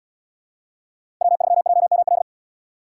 Morse code, heard as a keyed tone, about 77 wpm, 700 Hz. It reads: W3JML